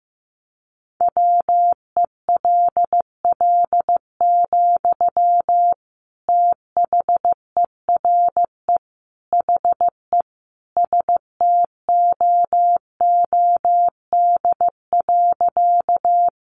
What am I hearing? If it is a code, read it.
Morse code, heard as a keyed tone, about 15 wpm, 700 Hz. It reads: WELL, THERE HE STOOD.
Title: Sentences from Top 500 Words 15wpm